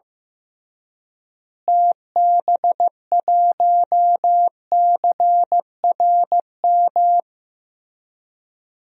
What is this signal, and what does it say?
Morse code, heard as a keyed tone, about 15 wpm, 700 Hz. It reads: TB1CRM
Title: DXCC-100 Call Signs - 6 Characters - Repeated 3x - 15wpm